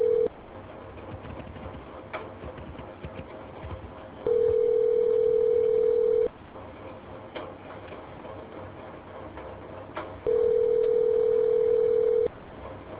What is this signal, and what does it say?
Ringback tone (North America), 440 + 480 Hz.